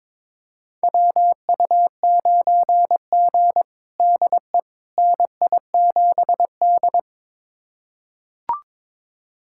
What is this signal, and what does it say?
Morse code, heard as a keyed tone, about 22 wpm, 700 Hz. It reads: WU9G DE NI7D
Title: USA Callsign DE Callsign - 4 Characters - 22wpm